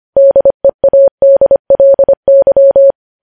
Morse code, heard as a keyed tone, about 25 wpm, 550 Hz. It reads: DEADLY
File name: podcast / News headlines-1x-25x25